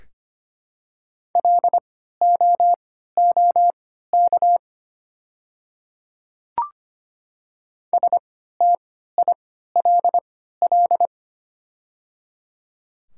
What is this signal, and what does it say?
Morse code, heard as a keyed tone, about 25 wpm, spaced as for 14 wpm, 700 Hz. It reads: LOOK STILL
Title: 3x Character Spacing - Top 300 Words - 25wpm